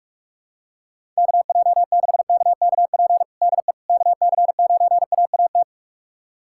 Morse code, heard as a keyed tone, about 46 wpm, 700 Hz. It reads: KJ6KKP DE KC9AAT
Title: USA Callsign DE Callsign - 6 Characters - 45wpm